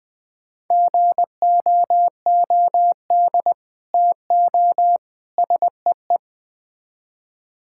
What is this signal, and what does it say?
Morse code, heard as a keyed tone, about 20 wpm, 700 Hz. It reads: GOOD TO SEE
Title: Build A Phrase - 4-Word Phrases - 20wpm